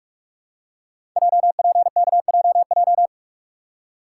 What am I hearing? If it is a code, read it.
Morse code, heard as a keyed tone, about 44 wpm, 700 Hz. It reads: JPKJJ
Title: Code Groups - 5 Letter:Numbers - 45wpm